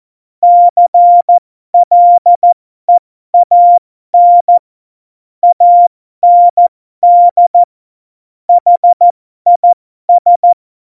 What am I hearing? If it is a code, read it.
Morse code, heard as a keyed tone, about 14 wpm, spaced as for 12 wpm, 700 Hz. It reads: CLEAN AND HIS
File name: Morsekurs / Witze